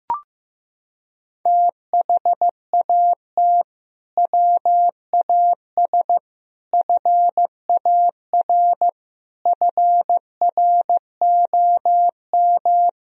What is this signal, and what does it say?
Morse code, heard as a keyed tone, about 15 wpm, 700 Hz. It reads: THAT WAS FAR FROM
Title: Sentences from Top 1000 Words - Review Part 3 - 15wpm